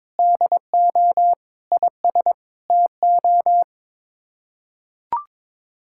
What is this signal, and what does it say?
Morse code, heard as a keyed tone, about 22 wpm, 700 Hz. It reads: DO IS TO
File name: practice / 22wpm / Sets of 7 Words - 22wpm